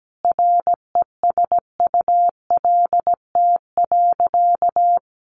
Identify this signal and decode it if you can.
Morse code, heard as a keyed tone, about 17 wpm, 700 Hz. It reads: RESULT.
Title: Sentences from Top 800 Words 17wpm